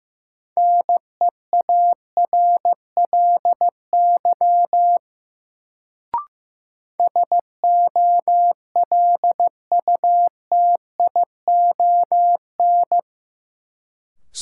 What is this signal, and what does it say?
Morse code, heard as a keyed tone, about 15 wpm, 700 Hz. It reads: NEARLY SOLUTION
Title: Sentences from Top 1300 Words - New Word Drill - 15wpm